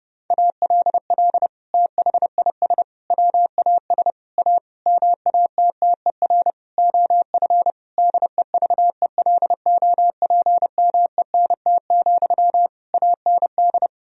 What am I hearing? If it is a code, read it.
Morse code, heard as a keyed tone, about 30 wpm, 700 Hz. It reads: ALL THIS WAS A MATTER OF DEVELOPMENT, AND
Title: Sentences from Top 1000 Words - Review Part 4 - 30wpm